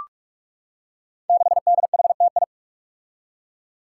Morse code, heard as a keyed tone, about 44 wpm, 700 Hz. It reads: 6DHTI
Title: Five Challenging Letter-Number Combinations 45wpm